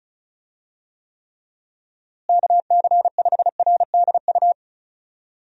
Morse code, heard as a keyed tone, about 34 wpm, 700 Hz. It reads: KC5RDU
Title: USA Call Signs - 6 Characters - Repeated 3x - 35wpm